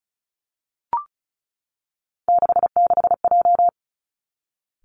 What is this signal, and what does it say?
Morse code, heard as a keyed tone, about 36 wpm, 700 Hz. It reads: 66J
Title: Three Challenging Letter-Number Combinations 35wpm